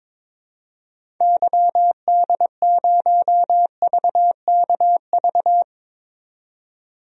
Morse code, heard as a keyed tone, about 22 wpm, 700 Hz. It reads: YD0VKV
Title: DXCC-100 Call Signs - 6 Characters - 22wpm